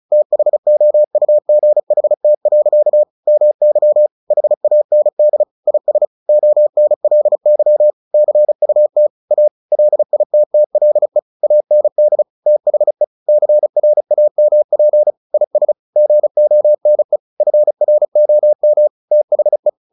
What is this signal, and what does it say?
Morse code, heard as a keyed tone, about 34 wpm, 600 Hz. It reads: THOUGHT. MY HAND IS ONLY CUT A LITTLE AND THE CRAMP IS GONE FROM THE
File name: the old man and the sea 2 35wpm